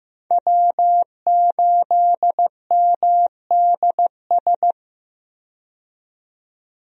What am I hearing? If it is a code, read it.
Morse code, heard as a keyed tone, about 15 wpm, 700 Hz. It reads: W8MDS
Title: USA Call Signs - 5 Characters - 15wpm